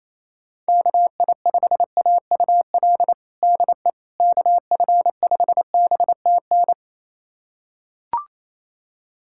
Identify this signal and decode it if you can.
Morse code, heard as a keyed tone, about 27 wpm, 700 Hz. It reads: KI5AUL DE KF5BTN
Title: USA Callsign DE Callsign - 6 Characters - 28wpm